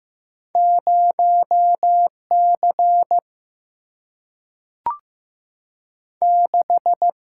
Morse code, heard as a keyed tone, about 15 wpm, 700 Hz. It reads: 0 C 6
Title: Single-Letter Single-Number 15wpm